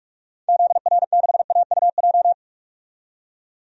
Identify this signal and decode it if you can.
Morse code, heard as a keyed tone, about 44 wpm, 690 Hz. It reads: ZR6AUJ
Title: DXCC-100 Call Signs - 6 Characters - 45wpm